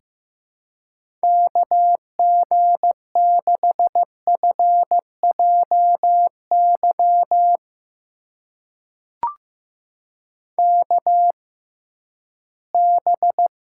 Morse code, heard as a keyed tone, about 15 wpm, spaced as for 11 wpm, 700 Hz. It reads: KG6FJY KB
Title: USA Call Signs - 6 Characters - 6x Character Spacing - 15wpm